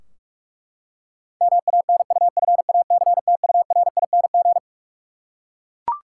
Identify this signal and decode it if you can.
Morse code, heard as a keyed tone, about 41 wpm, 700 Hz. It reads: MANUFACTURING
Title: Practice QSO - Exchange 3 - More Info - 45wpm